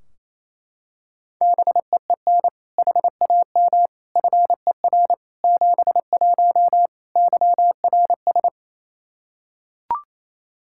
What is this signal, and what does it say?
Morse code, heard as a keyed tone, about 28 wpm, 700 Hz. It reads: BEEN HAM FER 71 YRS
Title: Practice QSO - Exchange 3 - More Info - 28wpm